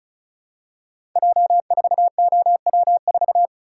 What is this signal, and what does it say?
Morse code, heard as a keyed tone, about 36 wpm, 700 Hz. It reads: J4OW4
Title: Code Groups - 5 Letter:Numbers - 35wpm